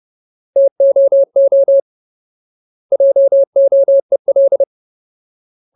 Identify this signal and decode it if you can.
Morse code, heard as a keyed tone, about 30 wpm, 550 Hz. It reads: TOO JOEL
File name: Ultimate Word List / 30WPM Ultimate